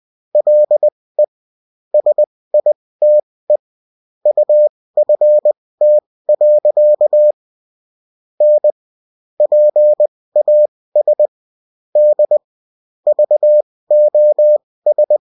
Morse code, heard as a keyed tone, about 20 wpm, spaced as for 16 wpm, 600 Hz. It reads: LE SITE UFT. N PAS D VOS